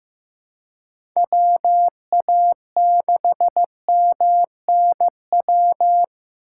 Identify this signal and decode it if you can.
Morse code, heard as a keyed tone, about 16 wpm, 700 Hz. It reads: WA6MNW
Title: USA Call Signs - 6 Characters - 2x Character Spacing - 15wpm